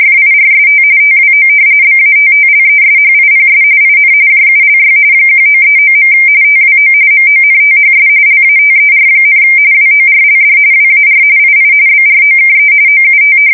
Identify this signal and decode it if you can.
RTTY, about 45.45 baud. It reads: FIRST TWO BATTLESHIPS IN JULY 1939. TWO MONTHS LATER, WORLD WAR II BROKE OUT
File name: fsk170 / battleshipnj